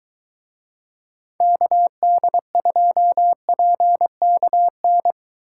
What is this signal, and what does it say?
Morse code, heard as a keyed tone, about 22 wpm, 700 Hz. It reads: KD2PKN